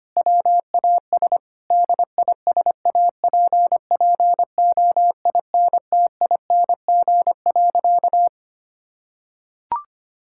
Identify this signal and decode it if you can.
Morse code, heard as a keyed tone, about 25 wpm, 700 Hz. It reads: WAS DISAPPOINTING.
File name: practice / 25wpm / Sentences with 4-Word Phrases - 25wpm